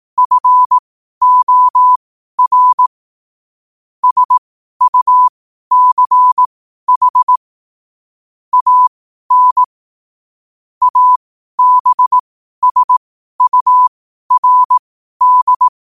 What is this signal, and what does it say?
Morse code, heard as a keyed tone, about 18 wpm, spaced as for 12 wpm, 1000 Hz. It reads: FOR SUCH AN ABSURD
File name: qotdfiles / 12wpm